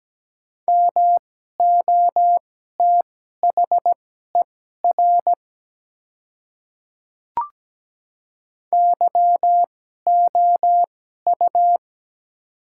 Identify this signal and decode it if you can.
Morse code, heard as a keyed tone, about 17 wpm, spaced as for 12 wpm, 700 Hz. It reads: MOTHER YOU
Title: 2x Character Spacing - Top 200 Words - 17wpm